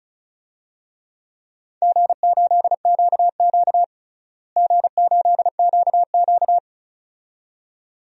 Morse code, heard as a keyed tone, about 36 wpm, 700 Hz. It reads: G8QQ G8QQ
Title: DXCC-100 Call Signs - 4 Characters - Repeated 2x - 35wpm